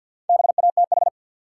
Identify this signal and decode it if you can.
Morse code, heard as a keyed tone, about 53 wpm, 700 Hz. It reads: BATH
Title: Sentences from Top 1600 Words - New Word Drill Alt - 50wpm